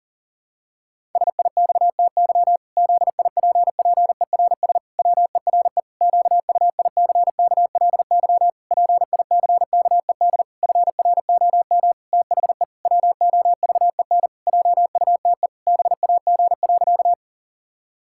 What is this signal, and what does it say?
Morse code, heard as a keyed tone, about 40 wpm, 700 Hz. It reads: SIXTY ZIPPERS WERE QUICKLY PICKED FROM THE WOVEN JUTE BAG.
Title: Pangram Challenge v2 - 40wpm